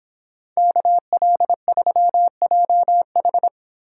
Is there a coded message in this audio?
Yes: Morse code from 0.6 to 3.5 s, heard as a keyed tone, about 25 wpm, 700 Hz. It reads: KL3JH